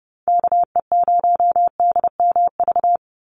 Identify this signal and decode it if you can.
Morse code, heard as a keyed tone, about 30 wpm, 700 Hz. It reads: KE0DMV
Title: USA Call Signs 30wpm